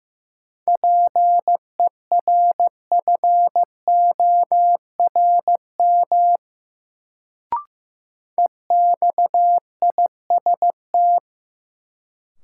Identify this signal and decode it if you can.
Morse code, heard as a keyed tone, about 15 wpm, 700 Hz. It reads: PERFORM EXIST
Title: Sentences from Top 1000 Words - Review Part 5 - New Word Drill - 15wpm